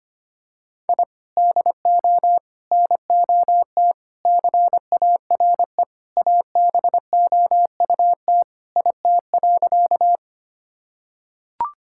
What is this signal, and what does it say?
Morse code, heard as a keyed tone, about 25 wpm, 700 Hz. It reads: I DO NOT CARE ABOUT IT.
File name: practice / 25wpm / Sentences with 4-Word Phrases - 25wpm